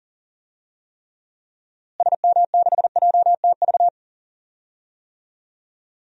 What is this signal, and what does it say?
Morse code, heard as a keyed tone, about 39 wpm, 700 Hz. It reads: SM6JTV
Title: DXCC-100 Call Signs - 6 Characters - Repeated 2x - 40wpm